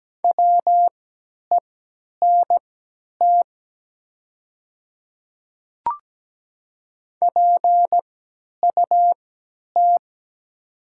Morse code, heard as a keyed tone, about 17 wpm, spaced as for 10 wpm, 700 Hz. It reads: WENT PUT
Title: 3x Character Spacing - Top 200 Words - 17wpm